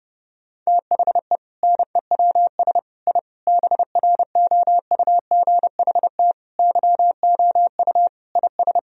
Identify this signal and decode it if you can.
Morse code, heard as a keyed tone, about 28 wpm, 700 Hz. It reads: THE NEWS I BROUGHT YOU IS
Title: Sentences from Top 1000 Words - Review Part 3 - 30wpm